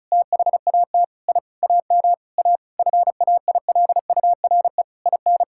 Morse code, heard as a keyed tone, about 36 wpm, 700 Hz. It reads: THAT I AM A FAILURE IN
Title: Sentences from Top 1000 Words - Review Part 6 - 35wpm